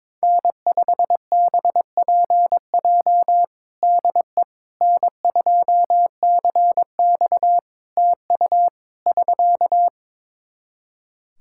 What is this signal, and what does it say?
Morse code, heard as a keyed tone, about 22 wpm, 700 Hz. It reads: N5BPJ DE N2CX TU <SK>
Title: K1USN SST QSO Practice 22wpm